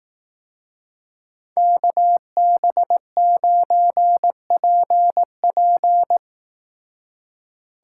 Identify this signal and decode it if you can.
Morse code, heard as a keyed tone, about 18 wpm, 700 Hz. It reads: KB9PP
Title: Speed Racing - USA 2x2 Call Signs 15wpm